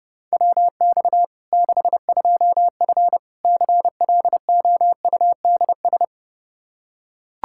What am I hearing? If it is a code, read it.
Morse code, heard as a keyed tone, about 30 wpm, 700 Hz. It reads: WX 62F CLOUDS
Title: K1USN SST QSO Practice 30wpm